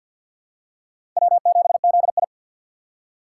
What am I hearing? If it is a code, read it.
Morse code, heard as a keyed tone, about 47 wpm, 700 Hz. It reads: W7ZI
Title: USA Call Signs - 4 Characters - 2x Character Spacing - 50wpm